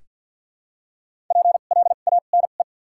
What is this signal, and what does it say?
Morse code, heard as a keyed tone, about 47 wpm, spaced as for 35 wpm, 700 Hz. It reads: PLANE
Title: 2x Character Spacing - Top 500 Words - 50wpm